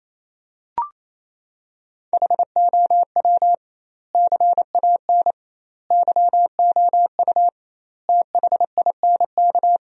Morse code, heard as a keyed tone, about 27 wpm, 700 Hz. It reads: HOW CAN YOU THINK
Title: 2x Word-Spacing - Sentences from Top 200 Words - 28wpm